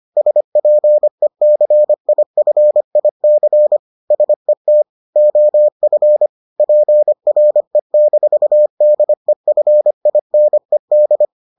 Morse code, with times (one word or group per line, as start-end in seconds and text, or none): 0.2-3.8 SPECIFIC
4.1-4.8 SET
5.2-6.3 OF
6.6-11.2 PRE-DEFINED